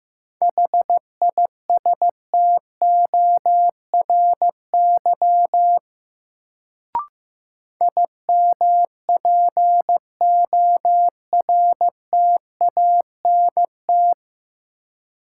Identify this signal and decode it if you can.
Morse code, heard as a keyed tone, about 15 wpm, 700 Hz. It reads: HISTORY IMPORTANT
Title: Sentences from Top 600 Words - New Word Drill - 15wpm